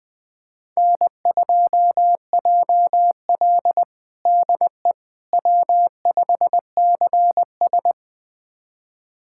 Morse code, heard as a keyed tone, about 20 wpm, 700 Hz. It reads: N2JL DE W5CS
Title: USA Callsign DE Callsign - 4 Characters - 20wpm